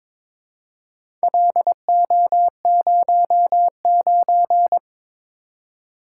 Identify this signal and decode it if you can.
Morse code, heard as a keyed tone, about 22 wpm, 700 Hz. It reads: LO09
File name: practice / 22wpm / Four Letter-Number Combinations - 22wpm